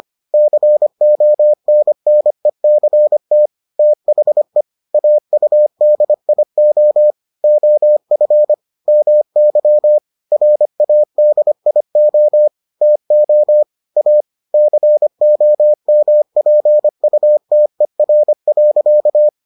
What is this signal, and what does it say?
Morse code, heard as a keyed tone, about 25 wpm, 600 Hz. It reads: CONNECT THE AUDIO OF MY RADIO TO A COMPUTER.